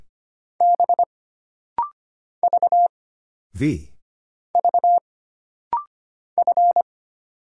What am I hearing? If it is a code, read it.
Morse code, heard as a keyed tone, about 25 wpm, 700 Hz. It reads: B V V F
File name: Single Character (Letter:Number:Punctuation:Prosign) - Rapid-Fire - 25wpm